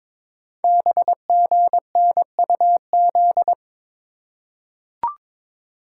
Morse code, heard as a keyed tone, about 22 wpm, 700 Hz. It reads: BGNUZ
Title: Code Groups - 5 Letters - 22wpm